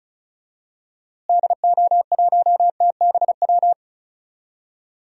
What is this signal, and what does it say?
Morse code, heard as a keyed tone, about 36 wpm, 700 Hz. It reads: DO1TBW
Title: DXCC-100 Call Signs - 6 Characters - Repeated 2x - 35wpm